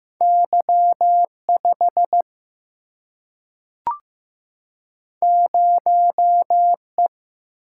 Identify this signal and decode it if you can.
Morse code, heard as a keyed tone, about 15 wpm, 700 Hz. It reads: Y5 0E
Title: Single-Letter Single-Number 15wpm